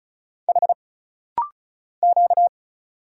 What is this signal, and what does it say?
Morse code, heard as a keyed tone, about 36 wpm, 700 Hz. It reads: H Q
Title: Single Letter-Number - Rapid-Fire 35wpm